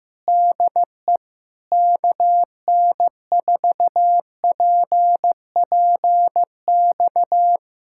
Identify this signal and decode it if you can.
Morse code, heard as a keyed tone, about 15 wpm, 700 Hz. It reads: DE KN4PPX
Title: Speed-Racing Common QSO Phrases Encore - 15wpm